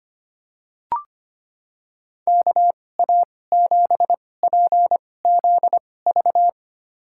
Morse code, heard as a keyed tone, about 25 wpm, spaced as for 18 wpm, 700 Hz. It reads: KA7PZV
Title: USA Call Signs - 6 Characters - 2x Character Spacing - 25wpm